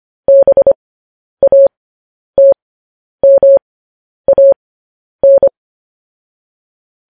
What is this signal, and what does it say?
Morse code, heard as a keyed tone, about 25 wpm, spaced as for 10 wpm, 550 Hz. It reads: BATMAN
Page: news headlines in morse code at 10 wpm